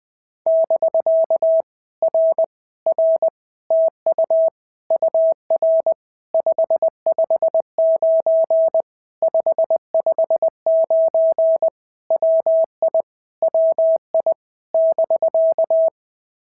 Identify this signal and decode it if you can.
Morse code, heard as a keyed tone, about 20 wpm, 650 Hz. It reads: <BK> R R TU UR 559 559 WI WI <BK>
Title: POTA QSO Practice - 20wpm